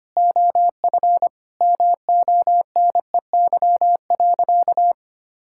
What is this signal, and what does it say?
Morse code, heard as a keyed tone, about 25 wpm, 700 Hz. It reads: OF MONEY.
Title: Sentences from Top 1000 Words - Review Part 6 - 25wpm